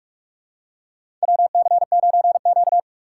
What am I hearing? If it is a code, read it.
Morse code, heard as a keyed tone, about 46 wpm, 700 Hz. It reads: WC9Q
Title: USA Call Signs - 6x Character Spacing - 45wpm